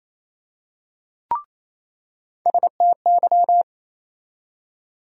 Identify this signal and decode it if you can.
Morse code, heard as a keyed tone, about 28 wpm, 700 Hz. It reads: STY
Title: USA Call Sign Postfixes - 3 Letters 28wpm